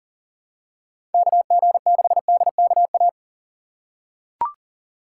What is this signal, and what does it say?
Morse code, heard as a keyed tone, about 40 wpm, 700 Hz. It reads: KG6DKA
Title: USA Call Signs - 6 Characters - 2x Character Spacing - 40wpm